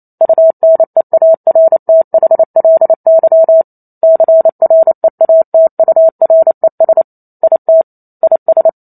Morse code, heard as a keyed tone, about 28 wpm, 650 Hz. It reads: UNEARTHLY CREATURES IT IS